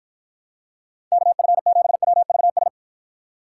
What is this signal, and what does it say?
Morse code, heard as a keyed tone, about 52 wpm, 700 Hz. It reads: KF7WVS